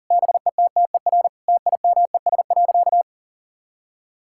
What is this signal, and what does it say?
Morse code, heard as a keyed tone, about 40 wpm, 700 Hz. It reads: BETTER TIMES.
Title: Sentences from Top 1000 Words - Review Part 6 - 40wpm